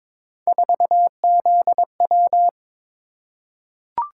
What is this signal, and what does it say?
Morse code, heard as a keyed tone, about 23 wpm, 700 Hz. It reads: 4ZW
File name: practice / 22wpm / Three Letter-Number Combinations - 22wpm